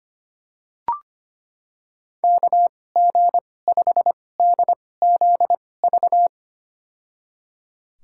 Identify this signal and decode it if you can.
Morse code, heard as a keyed tone, about 25 wpm, spaced as for 18 wpm, 700 Hz. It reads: KG5DZV
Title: USA Call Signs - 6 Characters - 2x Character Spacing - 25wpm